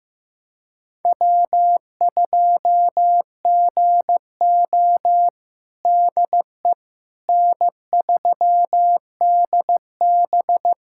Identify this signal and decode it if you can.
Morse code, heard as a keyed tone, about 15 wpm, 700 Hz. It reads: W2GO DE N3DB
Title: USA Callsign DE Callsign - 4 Characters - 15wpm